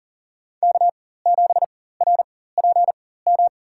Morse code, heard as a keyed tone, about 40 wpm, spaced as for 19 wpm, 700 Hz. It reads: K7RPM
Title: USA Call Signs - 4x Character Spacing - 40wpm